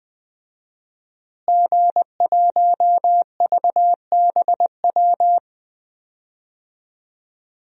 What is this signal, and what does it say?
Morse code, heard as a keyed tone, about 20 wpm, 700 Hz. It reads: G1VBW
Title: DXCC-100 Call Signs - 5 Characters - Repeated 2x - 20wpm